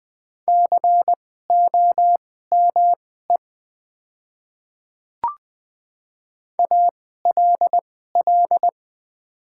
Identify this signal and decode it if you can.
Morse code, heard as a keyed tone, about 20 wpm, spaced as for 14 wpm, 700 Hz. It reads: COME ALL
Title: 2x Character Spacing - Top 200 Words - 20wpm